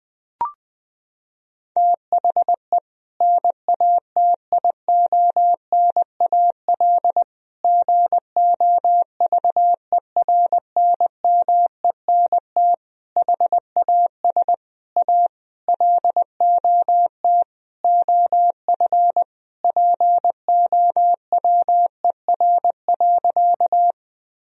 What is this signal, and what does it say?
Morse code, heard as a keyed tone, about 20 wpm, 700 Hz. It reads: THE NATIONAL GOVERNMENT HAS A LOT OF POWER.
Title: Sentences from Top 1000 Words - Review Part 4 - 20wpm